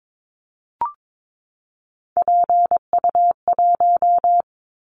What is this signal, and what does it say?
Morse code, heard as a keyed tone, about 22 wpm, 700 Hz. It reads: PU1
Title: Callsign Prefixes for Top 100 DXCC - 22wpm